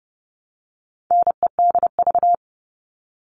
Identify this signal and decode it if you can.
Morse code, heard as a keyed tone, about 30 wpm, 700 Hz. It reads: NEDV
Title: All Cut Numbers - 4-Digit 30wpm